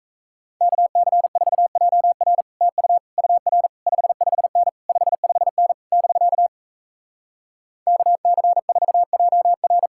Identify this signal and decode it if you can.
Morse code, heard as a keyed tone, about 37 wpm, 700 Hz. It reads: KC4JR TU UR 55N 55N <BK> KC4JR